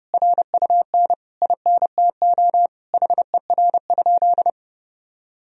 Morse code, heard as a keyed tone, about 30 wpm, 700 Hz. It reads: RUN INTO HER?
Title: Sentences from Top 1000 Words - Review Part 1 - 30wpm